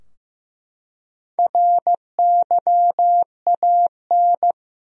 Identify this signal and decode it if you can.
Morse code, heard as a keyed tone, about 15 wpm, 700 Hz. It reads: RYAN
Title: USA Common Names - 15wpm